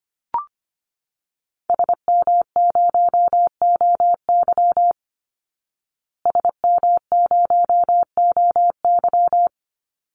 Morse code, heard as a keyed tone, about 25 wpm, 700 Hz. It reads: SM0OY SM0OY
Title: Call Signs Repeated 3x - 25wpm